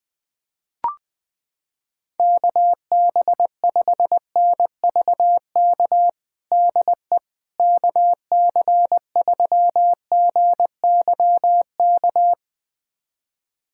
Morse code, heard as a keyed tone, about 20 wpm, 690 Hz. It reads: KB5NVK DE KC3GYK
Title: USA Callsign DE Callsign 20wpm